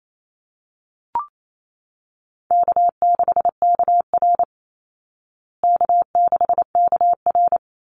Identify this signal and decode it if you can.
Morse code, heard as a keyed tone, about 28 wpm, 700 Hz. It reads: K6KR K6KR
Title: Easy Call Signs Repeated 2x - 28wpm